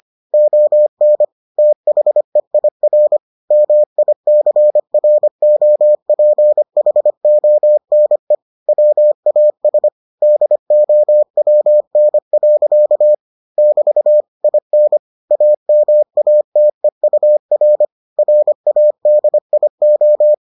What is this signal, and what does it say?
Morse code, heard as a keyed tone, about 25 wpm, 600 Hz. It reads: ON THEIR MICROPHONE WAS DOWN. <BT> IN AMATEUR RADIO